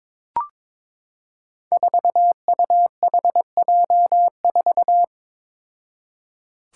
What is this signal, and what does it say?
Morse code, heard as a keyed tone, about 22 wpm, 700 Hz. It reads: 4UHJ4
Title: Five Challenging Letter-Number Combinations 22wpm